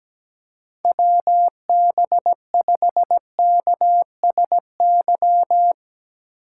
Morse code, heard as a keyed tone, about 17 wpm, 700 Hz. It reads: WB5KSY
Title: USA Call Signs - 6 Characters - 4x Character Spacing - 17wpm